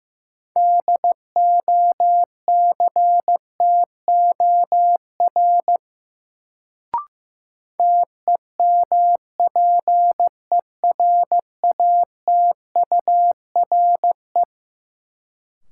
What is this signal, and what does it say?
Morse code, heard as a keyed tone, about 15 wpm, 700 Hz. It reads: DOCTOR TEMPERATURE
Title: Sentences from Top 1500 Words - New Word Drill Alt - 15wpm